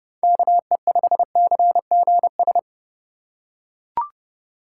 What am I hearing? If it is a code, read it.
Morse code, heard as a keyed tone, about 28 wpm, 700 Hz. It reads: KE5CGS